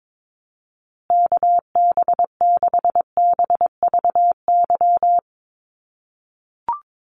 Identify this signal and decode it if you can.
Morse code, heard as a keyed tone, about 22 wpm, 700 Hz. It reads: KB6BVY
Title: USA Call Signs Repeated 2x 22wpm